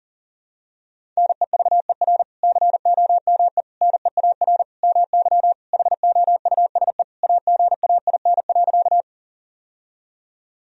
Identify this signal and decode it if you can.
Morse code, heard as a keyed tone, about 40 wpm, 700 Hz. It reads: NEVER COME NEAR MY HOUSE AGAIN.
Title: Sentences from Top 1000 Words - Review Part 1 - 40wpm